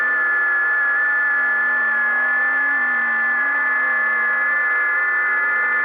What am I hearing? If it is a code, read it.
FT8.
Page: 5 min FT8 Test File